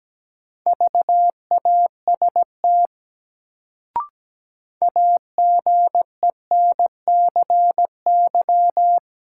Morse code, heard as a keyed tone, about 17 wpm, 700 Hz. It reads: VAST AGENCY
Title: Sentences from Top 1300 Words - New Word Drill - 17wpm